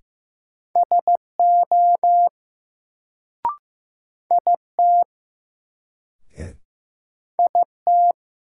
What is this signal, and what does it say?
Morse code, heard as a keyed tone, about 15 wpm, 700 Hz. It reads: SO IT IT
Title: Sentences from Top 100 Words - New Word Drill - 15wpm